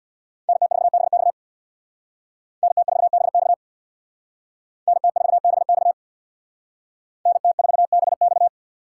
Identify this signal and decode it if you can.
Morse code, heard as a keyed tone, about 62 wpm, 700 Hz. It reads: NT4BX NT4BX NT4BX NT4BX
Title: Speed Racing - USA 2x2 Call Signs 50wpm